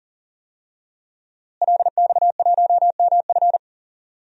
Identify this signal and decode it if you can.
Morse code, heard as a keyed tone, about 38 wpm, 700 Hz. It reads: LX1MF
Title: DXCC-100 Call Signs - 5 Characters - Repeated 2x - 40wpm